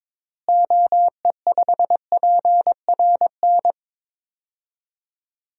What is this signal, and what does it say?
Morse code, heard as a keyed tone, about 22 wpm, 700 Hz. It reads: OE5PRN
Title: DXCC-100 Call Signs - 6 Characters - Repeated 3x - 22wpm